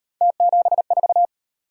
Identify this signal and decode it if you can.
Morse code, heard as a keyed tone, about 36 wpm, 700 Hz. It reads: T74